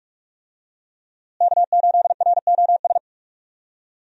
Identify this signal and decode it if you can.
Morse code, heard as a keyed tone, about 46 wpm, 700 Hz. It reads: K8ROS
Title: USA Call Signs - 5 Characters - Repeated 3x - 45wpm